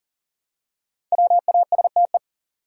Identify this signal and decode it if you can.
Morse code, heard as a keyed tone, about 40 wpm, 700 Hz. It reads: WASTE
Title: Five Letter Words - 40wpm